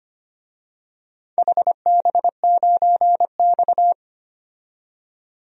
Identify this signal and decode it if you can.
Morse code, heard as a keyed tone, about 25 wpm, 700 Hz. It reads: HB9X
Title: DXCC-100 Call Signs - 4 Characters - 25wpm